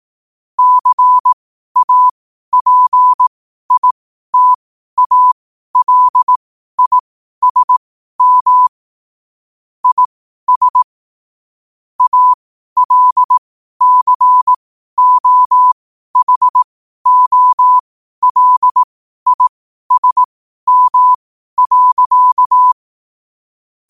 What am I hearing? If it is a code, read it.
Morse code, heard as a keyed tone, about 18 wpm, spaced as for 12 wpm, 1000 Hz. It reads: CAPITALISM IS ALCOHOLISM.
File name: qotdfiles / 12wpm